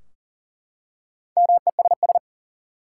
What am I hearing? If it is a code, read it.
Morse code, heard as a keyed tone, about 40 wpm, 700 Hz. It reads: MESS
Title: Four Letter Words - 40wpm